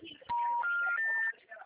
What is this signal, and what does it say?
Special information tone, with speech.
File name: out- -2021- - - .19397